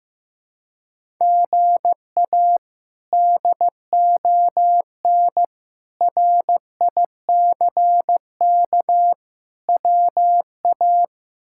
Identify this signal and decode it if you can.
Morse code, heard as a keyed tone, about 16 wpm, 700 Hz. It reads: GA DON RICK WA
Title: K1USN SST QSO Practice - 15wpm